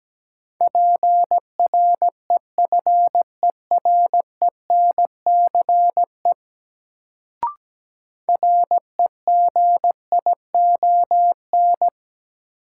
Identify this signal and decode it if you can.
Morse code, heard as a keyed tone, about 17 wpm, 700 Hz. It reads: PREFERENCE REGION
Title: Sentences from Top 1500 Words - New Word Drill - 17wpm